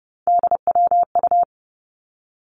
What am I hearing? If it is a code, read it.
Morse code, heard as a keyed tone, about 30 wpm, 700 Hz. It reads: DWU
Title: Three Challenging Letter-Number Combinations 30wpm